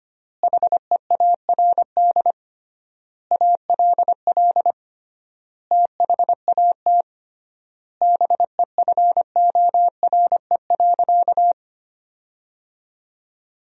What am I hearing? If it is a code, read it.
Morse code, heard as a keyed tone, about 25 wpm, 700 Hz. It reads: HEARD ALL THAT BEFORE.
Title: 3x Word-Spacing - Sentences from Top 400 Words - 25wpm